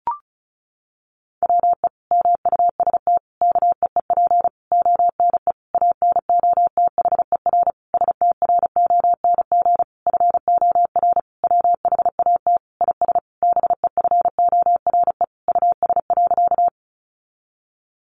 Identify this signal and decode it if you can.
Morse code, heard as a keyed tone, about 34 wpm, 700 Hz. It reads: WE MUST KEEP ONE ANOTHER STRONG FOR WHAT IS BEFORE US.
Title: Sentences from Top 600 Words 35wpm